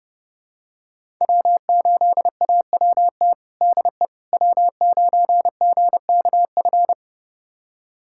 Morse code, heard as a keyed tone, about 32 wpm, 700 Hz. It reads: W8AWT DE W9GKF
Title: USA Callsign DE Callsign - 5 Characters - 30wpm